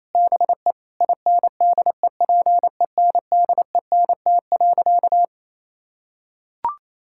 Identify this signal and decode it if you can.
Morse code, heard as a keyed tone, about 28 wpm, 700 Hz. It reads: BE INDEPENDENT.
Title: Sentences from Top 1000 Words - Review Part 6 - 28wpm